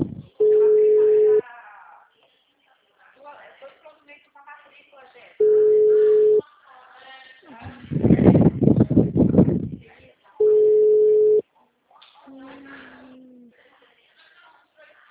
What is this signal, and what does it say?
Ringback tone (Europe, 425 Hz), with speech.